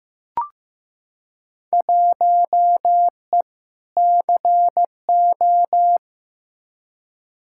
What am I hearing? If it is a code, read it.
Morse code, heard as a keyed tone, about 16 wpm, 700 Hz. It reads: 1E CO
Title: Field Day Exchanges - 15wpm